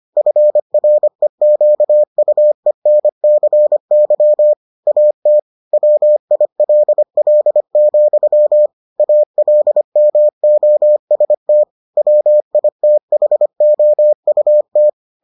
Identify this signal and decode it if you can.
Morse code, heard as a keyed tone, about 25 wpm, 600 Hz. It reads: FREQUENCY AT WILL, ALMOST WITHOUT